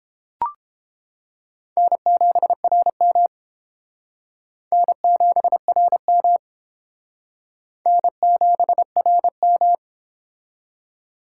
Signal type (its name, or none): Morse code